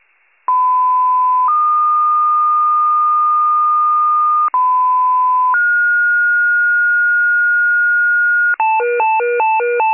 Two-tone page, heard as a one-second tone, then a three-second tone, about 1010 then 1220 Hz, about 1010 then 1430 Hz, with noise. It is heard from 0.5 to 8.5 s.
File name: pager / 94 FIRE / wcalert